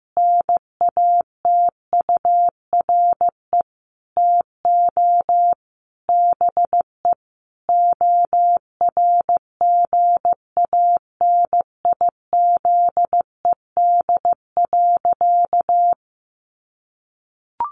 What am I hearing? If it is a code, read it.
Morse code, heard as a keyed tone, about 15 wpm, 700 Hz. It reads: NATURE TO BE ORGANIZED.
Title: Sentences from Top 1300 Words 15wpm